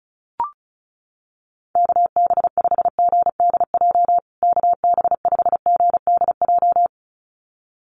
Morse code, heard as a keyed tone, about 34 wpm, 700 Hz. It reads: KB5GDJ KB5GDJ
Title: USA Call Signs Repeated 2x 35wpm